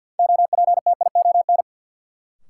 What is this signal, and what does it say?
Morse code, heard as a keyed tone, about 53 wpm, 700 Hz. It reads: OPTION
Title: Sentences from Top 1300 Words - New Word Drill - 50wpm